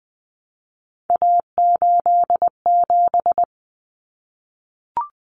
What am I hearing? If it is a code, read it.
Morse code, heard as a keyed tone, about 20 wpm, 700 Hz. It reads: A87
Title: Common Cut Numbers - 3-Digit 20wpm